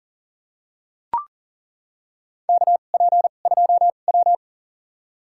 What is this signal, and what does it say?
Morse code, heard as a keyed tone, about 40 wpm, spaced as for 29 wpm, 700 Hz. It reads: KP2W
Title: USA Call Signs - 4 Characters - 2x Character Spacing - 40wpm